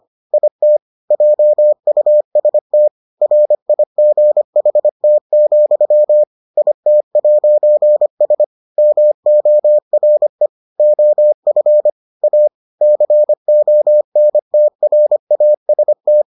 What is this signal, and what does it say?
Morse code, heard as a keyed tone, about 25 wpm, 600 Hz. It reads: IT JUST RIGHT, IT'S MORE OF A CONTRAST